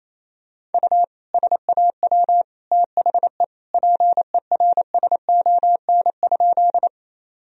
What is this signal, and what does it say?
Morse code, heard as a keyed tone, about 28 wpm, 700 Hz. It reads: U SAW THE PERSON?
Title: Sentences from Top 400 Words and Abbreviations 28wpm